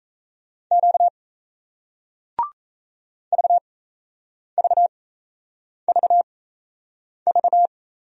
Morse code, heard as a keyed tone, about 37 wpm, spaced as for 8 wpm, 700 Hz. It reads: Q VVVV